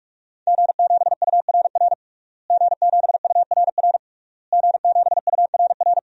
Morse code, heard as a keyed tone, about 41 wpm, 700 Hz. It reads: G7URR G7URR G7URR
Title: DXCC-100 Call Signs - 5 Characters - Repeated 3x - 45wpm